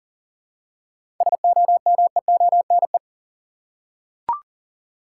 Morse code, heard as a keyed tone, about 40 wpm, 700 Hz. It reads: SOMEONE
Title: Seven Letter Words - 40wpm